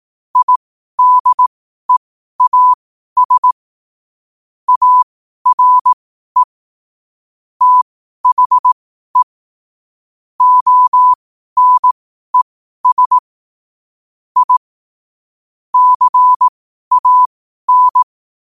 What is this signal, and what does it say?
Morse code, heard as a keyed tone, about 18 wpm, spaced as for 12 wpm, 1000 Hz. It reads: IDEAS ARE THE ONES I CAN
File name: qotdfiles / 12wpm